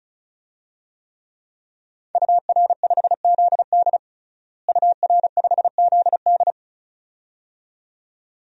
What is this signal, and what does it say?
Morse code, heard as a keyed tone, about 36 wpm, 700 Hz. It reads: UR5ZD UR5ZD
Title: DXCC-100 Call Signs - 5 Characters - Repeated 2x - 35wpm